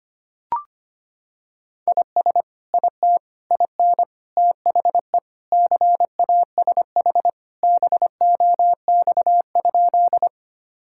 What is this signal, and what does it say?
Morse code, heard as a keyed tone, about 25 wpm, 700 Hz. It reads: IS IT IN THE CASH BOX?
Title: Sentences from Top 1000 Words - Review Part 5 - 25wpm